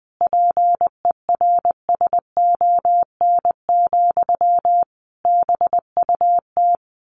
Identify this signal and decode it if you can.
Morse code, heard as a keyed tone, about 20 wpm, 700 Hz. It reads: PERSON, BUT
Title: Sentences from Top 1400 Words 20wpm